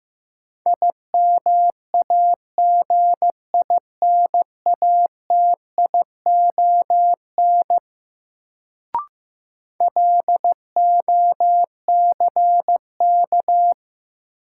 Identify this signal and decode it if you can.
Morse code, heard as a keyed tone, about 15 wpm, 700 Hz. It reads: IMAGINATION LOCK
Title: Sentences from Top 1300 Words - New Word Drill - 15wpm